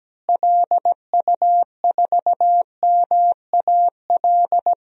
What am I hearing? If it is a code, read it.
Morse code, heard as a keyed tone, about 17 wpm, 700 Hz. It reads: LU4MAL
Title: DXCC-100 Call Signs - 6 Characters - Repeated 3x - 17wpm